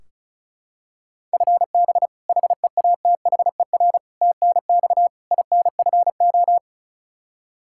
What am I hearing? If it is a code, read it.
Morse code, heard as a keyed tone, about 36 wpm, 700 Hz. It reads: FB HEATHER TNX INFO
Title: Practice QSO - Exchange 3 - More Info - 35wpm